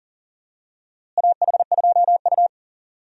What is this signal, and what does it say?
Morse code, heard as a keyed tone, about 40 wpm, 700 Hz. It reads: AH2U
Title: USA Call Signs - 4 Characters - Repeated 3x - 40wpm